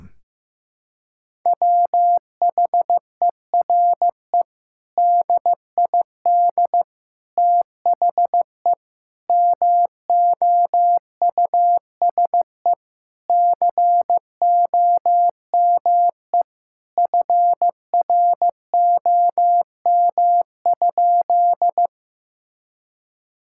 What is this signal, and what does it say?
Morse code, heard as a keyed tone, about 15 wpm, 700 Hz. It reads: WHERE DID THE MOUSE COME FROM?
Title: Sentences from Top 1000 Words - Review Part 5 - 15wpm